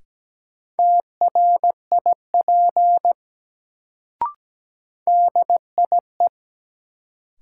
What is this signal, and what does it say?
Morse code, heard as a keyed tone, about 17 wpm, 700 Hz. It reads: TRIP DIE
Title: Sentences from Top 1300 Words - New Word Drill Alt - 17wpm